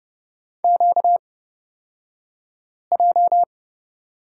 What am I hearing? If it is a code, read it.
Morse code, heard as a keyed tone, about 30 wpm, spaced as for 5 wpm, 700 Hz. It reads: QJ
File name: Lesson 38d - Letter q - Two Characters 30wpm